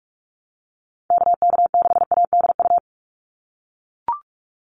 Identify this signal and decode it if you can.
Morse code, heard as a keyed tone, about 44 wpm, 700 Hz. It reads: KK6ADU
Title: USA Call Signs 45wpm